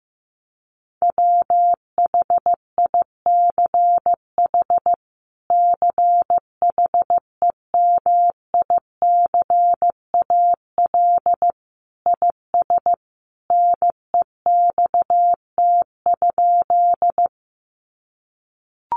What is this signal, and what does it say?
Morse code, heard as a keyed tone, about 15 wpm, 700 Hz. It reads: WHICH CHEMICAL IS NEXT?
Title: Sentences from Top 1200 Words 15wpm